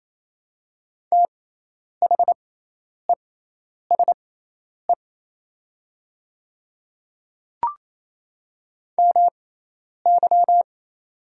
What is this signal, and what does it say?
Morse code, heard as a keyed tone, about 27 wpm, spaced as for 10 wpm, 700 Hz. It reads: THESE MY
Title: 6x Character Spacing - Top 100 Words - 28wpm